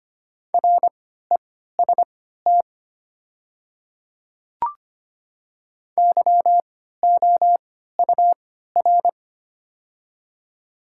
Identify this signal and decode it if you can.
Morse code, heard as a keyed tone, about 25 wpm, spaced as for 14 wpm, 700 Hz. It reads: REST YOUR
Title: 3x Character Spacing - Top 500 Words - 25wpm